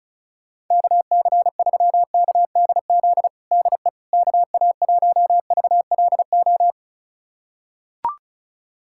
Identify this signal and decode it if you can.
Morse code, heard as a keyed tone, about 36 wpm, 700 Hz. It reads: KC3KDZ DE KA1VLO
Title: USA Callsign DE Callsign - 6 Characters - 35wpm